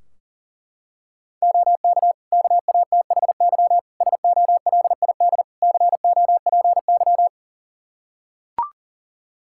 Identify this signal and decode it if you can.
Morse code, heard as a keyed tone, about 37 wpm, 700 Hz. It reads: OK KATHY SOLID COPY
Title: Practice QSO - Exchange 3 - More Info - 40wpm